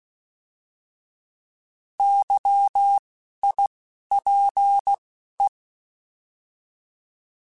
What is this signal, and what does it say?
Morse code, heard as a keyed tone, about 16 wpm, 800 Hz. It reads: Y I P E
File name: Alphabetical / Alphabetical mp3s / Ys @10